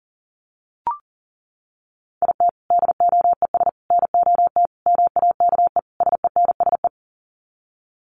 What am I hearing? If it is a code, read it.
Morse code, heard as a keyed tone, about 40 wpm, 700 Hz. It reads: IT DOES NOT MAKE SENSE
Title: Sets of 5 Words Encore2 40wpm